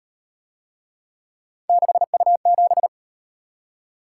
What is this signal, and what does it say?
Morse code, heard as a keyed tone, about 38 wpm, 700 Hz. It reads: 6U7